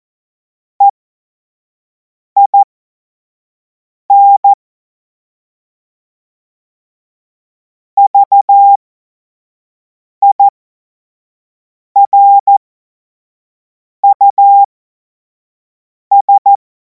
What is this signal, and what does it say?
Morse code, heard as a keyed tone, about 14 wpm, spaced as for 5 wpm, 800 Hz. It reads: EIN VIRUS
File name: Morsekurs / Witze / Witz 25. August 2026 25 F70